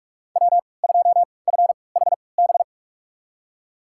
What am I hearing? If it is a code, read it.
Morse code, heard as a keyed tone, about 44 wpm, spaced as for 25 wpm, 700 Hz. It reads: W2FHB
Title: USA Call Signs - 5 Characters - 3x Character Spacing - 45wpm